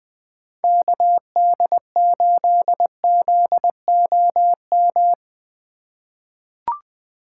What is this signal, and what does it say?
Morse code, heard as a keyed tone, about 20 wpm, 700 Hz. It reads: KD8ZOM
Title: USA Call Signs - 6 Characters - 2x Character Spacing - 20wpm